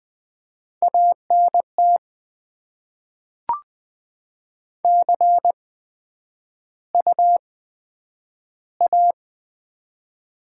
Morse code, heard as a keyed tone, about 20 wpm, spaced as for 11 wpm, 700 Hz. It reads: ANT CU A
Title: Top 20 QSO Elements - 8x Character Spacing - 20wpm